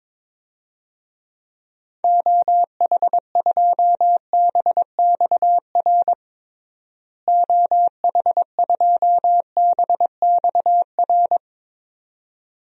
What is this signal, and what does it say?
Morse code, heard as a keyed tone, about 22 wpm, 700 Hz. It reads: OH2BXR OH2BXR
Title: DXCC-100 Call Signs - 6 Characters - Repeated 2x - 22wpm